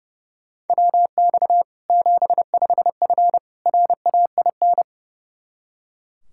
Morse code, heard as a keyed tone, about 30 wpm, 700 Hz. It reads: WX 75F RAIN
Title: Practice QSO - Exchange 2 - Info - 30wpm